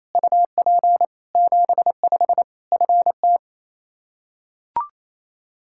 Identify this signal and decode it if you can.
Morse code, heard as a keyed tone, about 28 wpm, 700 Hz. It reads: UP 75 FT
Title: Practice QSO - Exchange 2 - Info - 28wpm